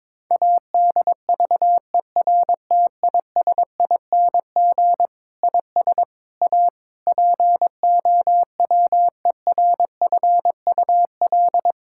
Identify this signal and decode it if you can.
Morse code, heard as a keyed tone, about 22 wpm, 700 Hz. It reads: ADVERTISING IS A POWERFUL
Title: Sentences from Top 1000 Words - Review Part 6 - 22wpm